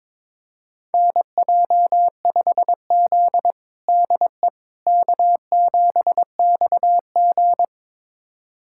Morse code, heard as a keyed tone, about 22 wpm, 700 Hz. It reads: NJ5Z DE K7XG
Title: USA Callsign DE Callsign - 4 Characters - 22wpm